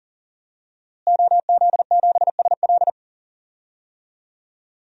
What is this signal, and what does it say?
Morse code, heard as a keyed tone, about 40 wpm, 700 Hz. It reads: OZ7SL
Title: DXCC-100 Call Signs - 5 Characters - Repeated 2x - 40wpm